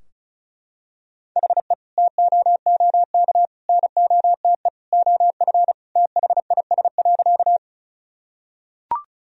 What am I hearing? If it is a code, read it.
Morse code, heard as a keyed tone, about 36 wpm, 700 Hz. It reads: HE TOOK NOTE OF THIS.
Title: Sentences from Top 1000 Words - Review Part 2 - 35wpm